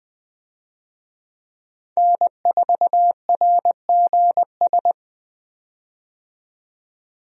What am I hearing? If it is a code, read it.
Morse code, heard as a keyed tone, about 20 wpm, 700 Hz. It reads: N4RGS
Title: USA Call Signs - 5 Characters - 20wpm